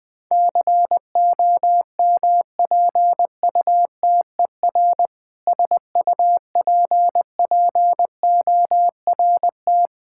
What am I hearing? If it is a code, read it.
Morse code, heard as a keyed tone, about 21 wpm, 700 Hz. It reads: COMPUTER SUPPORT
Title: Practice QSO - Exchange 3 - More Info - 20wpm